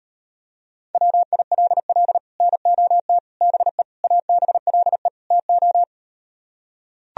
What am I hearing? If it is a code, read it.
Morse code, heard as a keyed tone, about 38 wpm, 700 Hz. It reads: WILL NOT BE ABLE TO